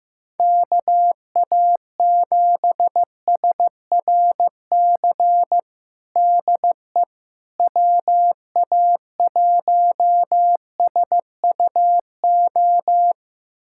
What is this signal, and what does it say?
Morse code, heard as a keyed tone, about 15 wpm, 700 Hz. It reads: KA7SRC DE WA1SUO
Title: USA Callsign DE Callsign - 6 Characters - 15wpm